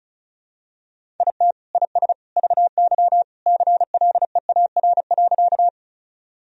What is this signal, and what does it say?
Morse code, heard as a keyed tone, about 34 wpm, 700 Hz. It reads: IT IS VY CLEAR.
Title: Sentences from Top 500 Words and Abbreviations 35wpm